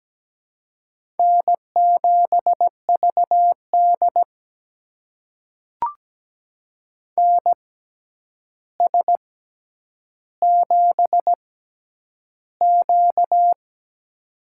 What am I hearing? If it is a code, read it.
Morse code, heard as a keyed tone, about 17 wpm, spaced as for 9 wpm, 700 Hz. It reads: N7VD NS7Q
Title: USA Call Signs - 4 Characters - 6x Character Spacing - 17wpm